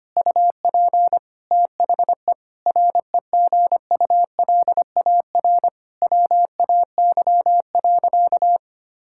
Morse code, heard as a keyed tone, about 25 wpm, 700 Hz. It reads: UP THE REGULAR WAY.
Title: Sentences from Top 1000 Words - Review Part 6 - 25wpm